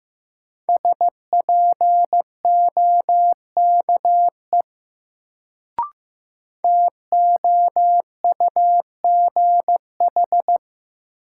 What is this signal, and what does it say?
Morse code, heard as a keyed tone, about 15 wpm, 700 Hz. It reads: SPOKE TOUGH
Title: Sentences from Top 1200 Words - New Word Drill - 15wpm